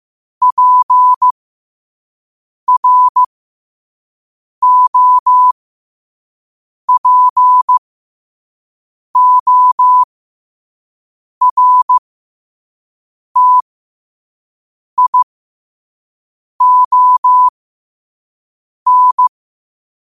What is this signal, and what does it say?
Morse code, heard as a keyed tone, about 15 wpm, spaced as for 5 wpm, 1000 Hz. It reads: PROPORTION
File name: qotdfiles / 5wpm